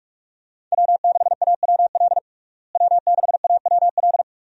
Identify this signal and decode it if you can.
Morse code, heard as a keyed tone, about 46 wpm, 700 Hz. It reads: W6AWL W6AWL
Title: USA Call Signs - 5 Characters - Repeated 2x - 45wpm